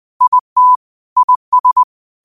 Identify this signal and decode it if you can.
Morse code, heard as a keyed tone, about 20 wpm, 1000 Hz. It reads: IT IS
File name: qotdfiles / 20wpm